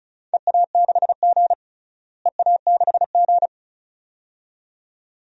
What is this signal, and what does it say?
Morse code, heard as a keyed tone, about 36 wpm, 690 Hz. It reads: EA6G EA6G
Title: DXCC-100 Call Signs - 4 Characters - Repeated 2x - 35wpm